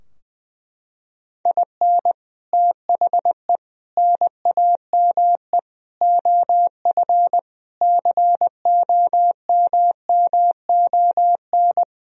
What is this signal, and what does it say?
Morse code, heard as a keyed tone, about 21 wpm, 700 Hz. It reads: IN THE NAME OF COMMON
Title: Sentences from Top 1000 Words - Review Part 4 - 20wpm